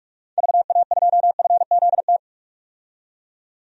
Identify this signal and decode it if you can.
Morse code, heard as a keyed tone, about 46 wpm, 700 Hz. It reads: VA2FZT
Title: DXCC-100 Call Signs - 6 Characters - 45wpm